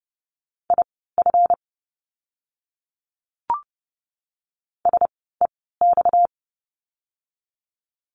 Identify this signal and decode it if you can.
Morse code, heard as a keyed tone, about 30 wpm, spaced as for 17 wpm, 700 Hz. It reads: IF SEX
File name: course / 30:10wpm / 230 Lesson 37f - Letter x - Words 30wpm